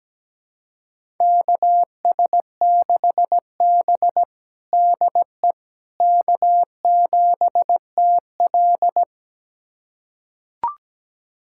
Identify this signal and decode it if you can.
Morse code, heard as a keyed tone, about 17 wpm, 700 Hz. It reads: KS6B DE K7TL
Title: USA Callsign DE Callsign - 4 Characters - 17wpm